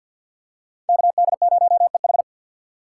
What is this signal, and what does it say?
Morse code, heard as a keyed tone, about 53 wpm, 700 Hz. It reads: KD0EH
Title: USA Call Signs - 3x Character Spacing - 50wpm